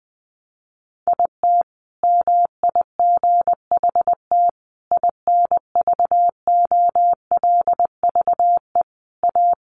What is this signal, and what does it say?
Morse code, heard as a keyed tone, about 20 wpm, 700 Hz. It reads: IT MIGHT INVOLVE A
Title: Sentences from Top 1200 Words 20wpm